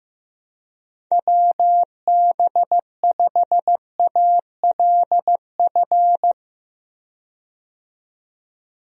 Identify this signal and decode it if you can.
Morse code, heard as a keyed tone, about 15 wpm, 700 Hz. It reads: WB5ALF
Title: USA Call Signs - 6 Characters - 15wpm